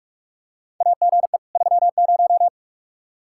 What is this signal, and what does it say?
Morse code, heard as a keyed tone, about 47 wpm, 700 Hz. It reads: AGE 30
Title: Practice QSO - Exchange 3 - More Info - 45wpm